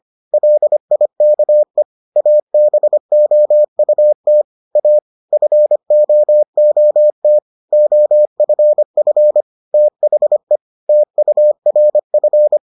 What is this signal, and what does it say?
Morse code, heard as a keyed tone, about 25 wpm, 600 Hz. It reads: LIKE ABOUT A FOOT OFF THE TURF